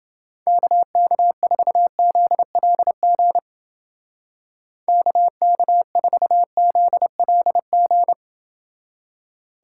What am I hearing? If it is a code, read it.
Morse code, heard as a keyed tone, about 30 wpm, 700 Hz. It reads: KK4ZLG KK4ZLG